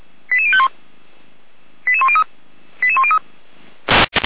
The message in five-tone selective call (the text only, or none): 88051 88103 88103